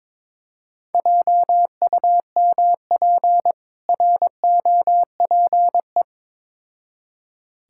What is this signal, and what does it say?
Morse code, heard as a keyed tone, about 22 wpm, 690 Hz. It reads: JUMP ROPE
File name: Difficult Words with Hints - 22wpm